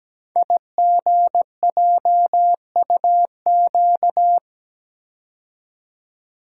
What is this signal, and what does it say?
Morse code, heard as a keyed tone, about 17 wpm, 700 Hz. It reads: IGJUQ
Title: Code Groups - 5 Letters - 17wpm